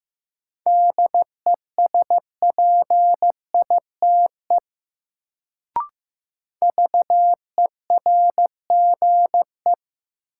Morse code, heard as a keyed tone, about 15 wpm, 700 Hz. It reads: DESPITE VERGE
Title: Sets of 4 Words - New Word Drill - 15wpm